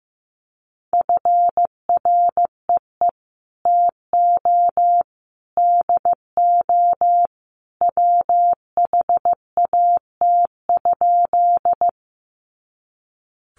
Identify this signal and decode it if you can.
Morse code, heard as a keyed tone, about 15 wpm, 700 Hz. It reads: FREE TO DO WHAT?
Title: Sentences from Top 500 Words 15wpm